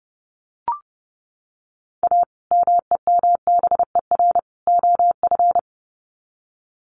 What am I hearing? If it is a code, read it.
Morse code, heard as a keyed tone, about 30 wpm, 700 Hz. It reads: A MEMBER OF
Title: Sets of 3 Words 30wpm